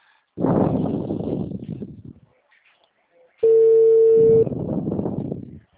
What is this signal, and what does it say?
Ringback tone (China, 450 Hz).
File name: out- -2018